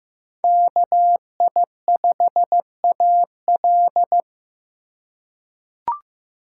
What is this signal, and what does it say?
Morse code, heard as a keyed tone, about 15 wpm, 700 Hz. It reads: KI5AL
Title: K1USN SST QSO Practice - 15wpm